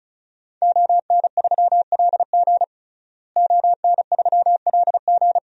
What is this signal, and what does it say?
Morse code, heard as a keyed tone, about 36 wpm, 700 Hz. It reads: ON3LG ON3LG
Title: DXCC-100 Call Signs - 5 Characters - Repeated 2x - 35wpm